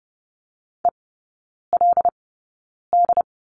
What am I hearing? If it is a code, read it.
Morse code, heard as a keyed tone, about 30 wpm, spaced as for 9 wpm, 700 Hz. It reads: ELD
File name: Lesson 16e - Letter c - Three Characters 30wpm